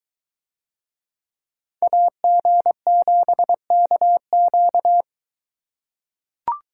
Morse code, heard as a keyed tone, about 21 wpm, 700 Hz. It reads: AG7KQ